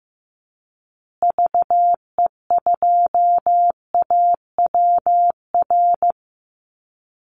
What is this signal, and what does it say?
Morse code, heard as a keyed tone, about 15 wpm, 700 Hz. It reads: VE2AWR
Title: Call Signs Repeated 3x - 15wpm